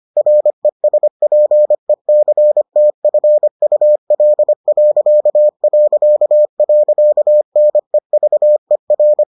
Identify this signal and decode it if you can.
Morse code, heard as a keyed tone, about 25 wpm, 600 Hz. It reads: RESPECTFUL...NEVER